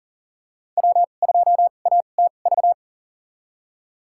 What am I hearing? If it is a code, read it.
Morse code, heard as a keyed tone, about 40 wpm, spaced as for 29 wpm, 700 Hz. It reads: W2ATV
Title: USA Call Signs - 5 Characters - 2x Character Spacing - 40wpm